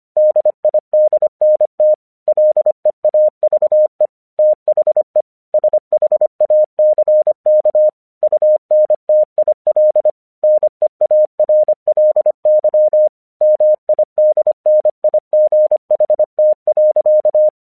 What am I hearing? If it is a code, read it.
Morse code, heard as a keyed tone, about 24 wpm, 610 Hz. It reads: DIDNT LEAVE THE SHACK UNTIL NEARLY MIDNIGHT.
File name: 1 English 25wpm / AQC24